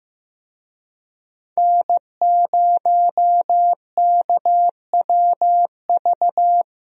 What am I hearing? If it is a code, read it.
Morse code, heard as a keyed tone, about 15 wpm, 700 Hz. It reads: N0KWV
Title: USA Call Signs - 5 Characters - Repeated 3x - 15wpm